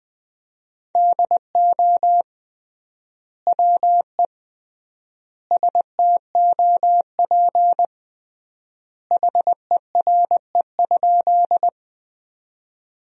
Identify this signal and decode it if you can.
Morse code, heard as a keyed tone, about 20 wpm, 700 Hz. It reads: DO WE STOP HERE?
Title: 3x Word-Spacing - Sentences from Top 300 Words - 20wpm